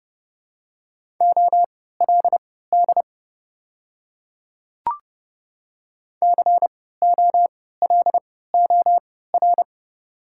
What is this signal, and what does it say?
Morse code, heard as a keyed tone, about 30 wpm, spaced as for 17 wpm, 700 Hz. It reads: OLD COLOR
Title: 3x Character Spacing - Top 300 Words - 30wpm